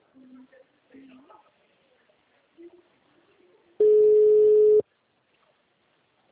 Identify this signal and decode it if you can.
Ringback tone (Europe, 425 Hz), with speech.
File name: out- -2001